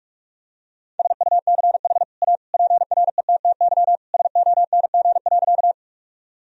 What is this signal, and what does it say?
Morse code, heard as a keyed tone, about 46 wpm, 700 Hz. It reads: SUCH A PRETTY SONG.
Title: Sentences from Top 1000 Words - Review Part 2 - 45wpm